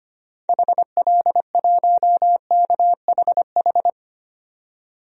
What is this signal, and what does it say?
Morse code, heard as a keyed tone, about 25 wpm, 700 Hz. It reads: HL1KHH
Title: DXCC-100 Call Signs - 6 Characters - 25wpm